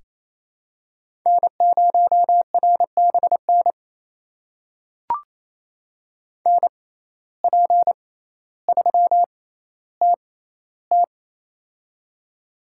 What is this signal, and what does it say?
Morse code, heard as a keyed tone, about 28 wpm, spaced as for 14 wpm, 700 Hz. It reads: N0RBN NP3TT
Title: USA Call Signs - 5 Characters - 6x Character Spacing - 28wpm